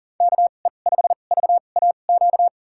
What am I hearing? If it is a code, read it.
Morse code, heard as a keyed tone, about 40 wpm, spaced as for 29 wpm, 700 Hz. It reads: KE5VAQ
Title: USA Call Signs - 6 Characters - 2x Character Spacing - 40wpm